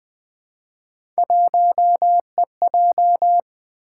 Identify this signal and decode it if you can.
Morse code, heard as a keyed tone, about 20 wpm, 700 Hz. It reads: 1EJ
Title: Three Letter-Number Combinations - 20wpm